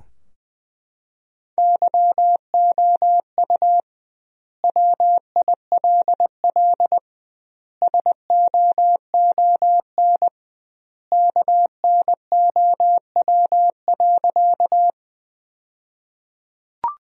Morse code, heard as a keyed tone, about 21 wpm, 700 Hz. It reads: YOU WILL SOON KNOW.
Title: 2x Word-Spacing - Sentences from Top 400 Words - 20wpm